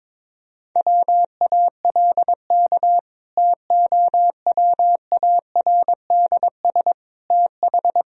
Morse code, heard as a keyed tone, about 21 wpm, 700 Hz. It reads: WALK TOWARDS TH
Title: Sentences from Top 1000 Words - Review Part 2 - 22wpm